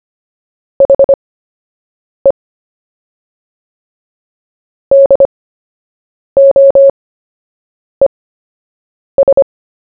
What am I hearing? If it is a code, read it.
Morse code, heard as a keyed tone, about 25 wpm, spaced as for 7 wpm, 550 Hz. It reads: HE DOES
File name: cw / fists / Short-Story-1-7wpm